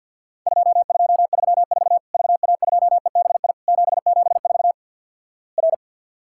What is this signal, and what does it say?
Morse code, heard as a keyed tone, about 50 wpm, 700 Hz. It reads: 2234 VA2EBI 774 R
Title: Worked All Europe DX Contest - QTC Exchange - 50wpm